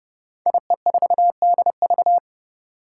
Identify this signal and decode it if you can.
Morse code, heard as a keyed tone, about 30 wpm, 700 Hz. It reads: IE4DV
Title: Five Challenging Letter-Number Combinations 30wpm